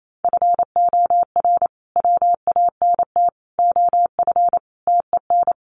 Morse code, heard as a keyed tone, about 28 wpm, 700 Hz. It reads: FOR WANT OF TEN
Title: A Sound Of Thunder 28wpm